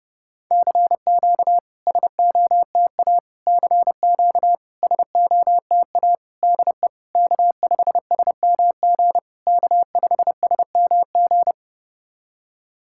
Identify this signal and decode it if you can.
Morse code, heard as a keyed tone, about 30 wpm, 700 Hz. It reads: CQ SOTA CQ SOTA DE K5SMG K5SMG
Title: SOTA QSO Practice 30wpm